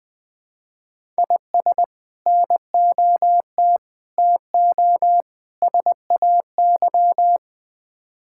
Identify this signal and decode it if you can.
Morse code, heard as a keyed tone, about 21 wpm, 700 Hz. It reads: IS NOT TO SAY
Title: Build A Phrase - 5-Word Phrases - 20wpm